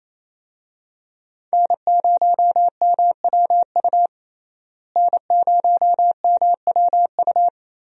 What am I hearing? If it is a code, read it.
Morse code, heard as a keyed tone, about 28 wpm, 700 Hz. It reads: N0MWU N0MWU
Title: USA Call Signs - 5 Characters - Repeated 3x - 28wpm